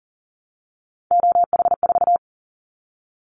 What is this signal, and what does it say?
Morse code, heard as a keyed tone, about 41 wpm, 700 Hz. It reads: OH4
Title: Callsign Prefixes for Top 50 DXCC - 40wpm